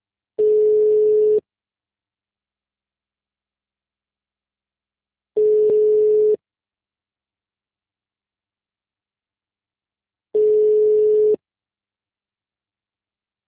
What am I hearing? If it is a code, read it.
Ringback tone (Europe, 425 Hz).
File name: out- -2001- - - .15